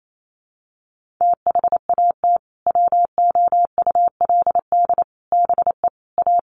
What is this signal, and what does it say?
Morse code, heard as a keyed tone, about 28 wpm, 700 Hz. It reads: THAT WOULD BE A
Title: Sentences from Top 500 Words 28wpm